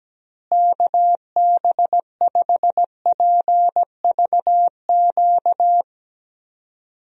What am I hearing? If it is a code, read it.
Morse code, heard as a keyed tone, about 17 wpm, 700 Hz. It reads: KB5PVQ
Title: USA Call Signs - 6 Characters - Repeated 3x - 17wpm